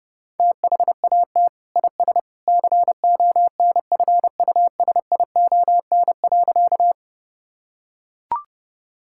Morse code, heard as a keyed tone, about 30 wpm, 700 Hz. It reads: THAT IS CONFUSION.
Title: Sentences from Top 1600 Words 30wpm